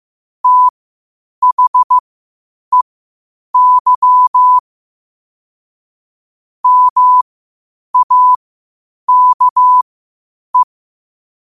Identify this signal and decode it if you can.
Morse code, heard as a keyed tone, about 15 wpm, spaced as for 8 wpm, 1000 Hz. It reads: THEY MAKE
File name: qotdfiles / 8wpm